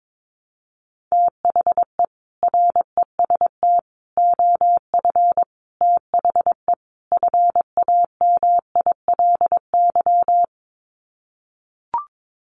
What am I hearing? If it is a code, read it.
Morse code, heard as a keyed tone, about 22 wpm, 700 Hz. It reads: THE REST OF THE FAMILY
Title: Sets of 5 Words Encore2 22wpm